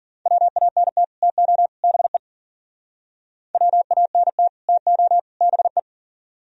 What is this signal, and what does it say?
Morse code, heard as a keyed tone, about 55 wpm, 700 Hz. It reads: WANT TO BE WANT TO BE